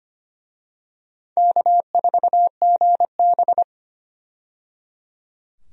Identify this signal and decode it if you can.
Morse code, heard as a keyed tone, about 25 wpm, 700 Hz. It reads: K4GB
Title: USA Call Signs - 4 Characters - 25wpm